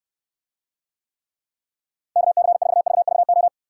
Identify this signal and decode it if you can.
Morse code, heard as a keyed tone, about 69 wpm, 700 Hz. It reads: KC4VVO